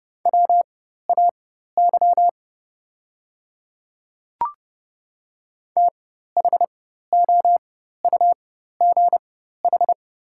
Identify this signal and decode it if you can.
Morse code, heard as a keyed tone, about 30 wpm, spaced as for 14 wpm, 700 Hz. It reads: WAY THOUGH
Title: 4x Character Spacing - Top 500 Words - 30wpm